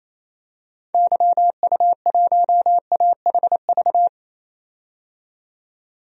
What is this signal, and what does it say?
Morse code, heard as a keyed tone, about 28 wpm, 700 Hz. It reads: YU1AHV
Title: DXCC-100 Call Signs - 6 Characters - 28wpm